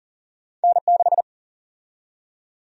Morse code, heard as a keyed tone, about 38 wpm, 700 Hz. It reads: N6
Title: USA Call Sign Prefixes - Repeated 2x - 40wpm